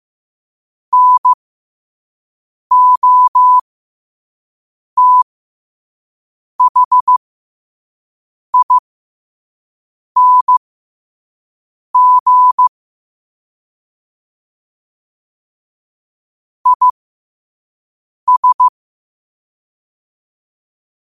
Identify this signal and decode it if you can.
Morse code, heard as a keyed tone, about 15 wpm, spaced as for 5 wpm, 1000 Hz. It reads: NOTHING IS
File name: qotdfiles / 5wpm